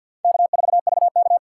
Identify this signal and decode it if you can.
Morse code, heard as a keyed tone, about 53 wpm, 700 Hz. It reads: K4VK
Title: USA Call Signs - 4 Characters - 6x Character Spacing - 50wpm